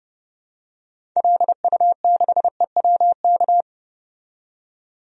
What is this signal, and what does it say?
Morse code, heard as a keyed tone, about 30 wpm, 700 Hz. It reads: LU6EWK
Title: DXCC-100 Call Signs - 6 Characters - 30wpm